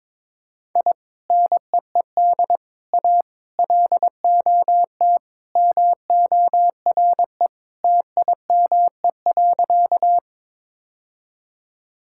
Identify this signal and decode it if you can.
Morse code, heard as a keyed tone, about 22 wpm, 700 Hz. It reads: I NEED A LOT MORE TIME.
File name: practice / 22wpm / Sentences with 3-Word Phrases - 22wpm